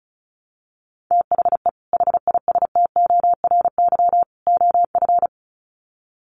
Morse code, heard as a keyed tone, about 34 wpm, 700 Hz. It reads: THE HISTORY OF
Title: Sets of 3 Words 35wpm